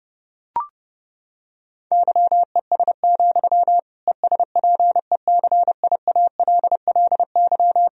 Morse code, heard as a keyed tone, about 30 wpm, 700 Hz. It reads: YES, ESPECIALLY
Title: Sentences from Top 1000 Words - Review Part 4 - 30wpm